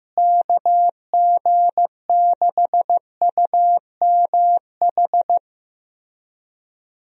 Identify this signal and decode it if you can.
Morse code, heard as a keyed tone, about 15 wpm, 700 Hz. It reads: KG6UMH
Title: USA Call Signs - 6 Characters - Repeated 2x - 15wpm